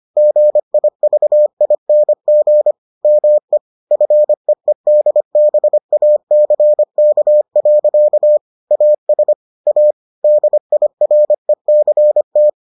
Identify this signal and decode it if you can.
Morse code, heard as a keyed tone, about 25 wpm, 600 Hz. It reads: GIVING ME FEEDBACK. AS A DIRECT